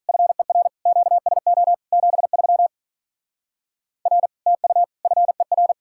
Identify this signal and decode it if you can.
Morse code, heard as a keyed tone, about 43 wpm, 700 Hz. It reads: FER QSO 73 R TU FER
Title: Speed-Racing Common QSO Phrases - 35wpm